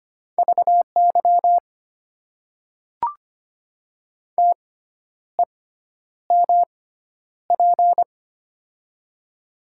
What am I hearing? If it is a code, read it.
Morse code, heard as a keyed tone, about 25 wpm, spaced as for 10 wpm, 700 Hz. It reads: VY TEMP
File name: Top 90 QSO Elements - 6x Character Spacing - 25wpm